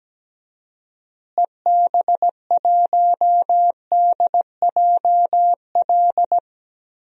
Morse code, heard as a keyed tone, about 17 wpm, 700 Hz. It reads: EB1DJL
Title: DXCC-100 Call Signs - 6 Characters - Repeated 2x - 17wpm